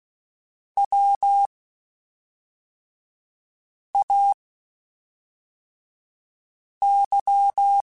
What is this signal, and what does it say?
Morse code, heard as a keyed tone, about 16 wpm, spaced as for 3 wpm, 800 Hz. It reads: WAY
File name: (spaced) / QSOs Spaced-mp3s / WA7QOX 1 @5 sp